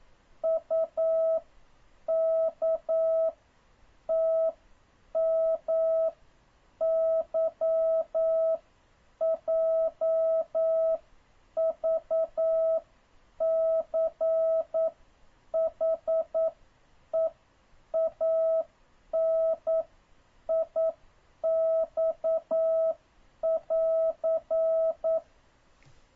Morse code, heard as a keyed tone, about 9 wpm, spaced as for 7 wpm, 650 Hz. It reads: UKTMYJVCHEANIX<AR>
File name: jh8cbh / 2026cw